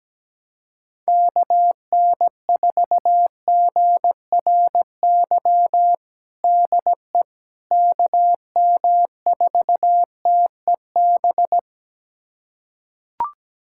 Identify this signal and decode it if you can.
Morse code, heard as a keyed tone, about 17 wpm, 700 Hz. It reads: KN4GRY DE KM4TEB
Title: USA Callsign DE Callsign 17wpm